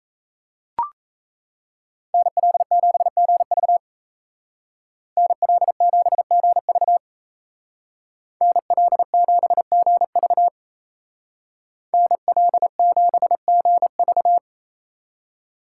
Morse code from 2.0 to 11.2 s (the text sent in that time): NL7GV NL7GV NL7GV